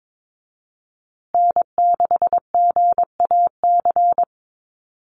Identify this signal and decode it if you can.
Morse code, heard as a keyed tone, about 22 wpm, 700 Hz. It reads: N6GAC
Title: USA Call Signs Repeated 3x 22wpm